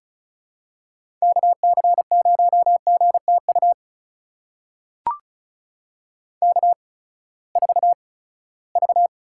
Morse code, heard as a keyed tone, about 36 wpm, spaced as for 20 wpm, 700 Hz. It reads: KC0GTU K4V
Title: USA Call Signs - 8x Character Spacing - 35wpm